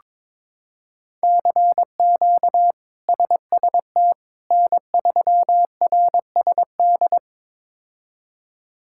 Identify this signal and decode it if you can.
Morse code, heard as a keyed tone, about 22 wpm, 700 Hz. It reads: CQ SST N3RSD
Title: K1USN SST QSO Practice - 22wpm